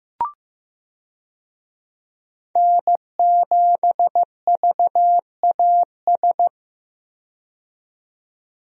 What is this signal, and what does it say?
Morse code, heard as a keyed tone, about 15 wpm, 700 Hz. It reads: N7VAS
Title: USA Call Signs - 5 Characters - 15wpm